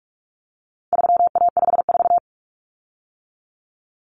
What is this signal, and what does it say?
Morse code, heard as a keyed tone, about 44 wpm, 700 Hz. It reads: 3A54
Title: Common Cut Numbers - 4-Digit 45wpm